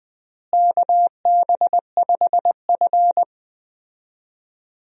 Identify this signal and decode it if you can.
Morse code, heard as a keyed tone, about 21 wpm, 700 Hz. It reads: KB5F
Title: USA Call Signs - 4 Characters - Repeated 3x - 20wpm